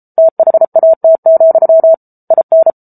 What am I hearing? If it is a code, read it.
Morse code, heard as a keyed tone, about 34 wpm, 650 Hz. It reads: THAT, IN